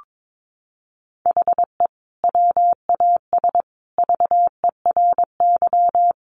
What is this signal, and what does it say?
Morse code, heard as a keyed tone, about 22 wpm, 700 Hz. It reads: HE WAS VERY
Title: Sentences from Top 1400 Words 22wpm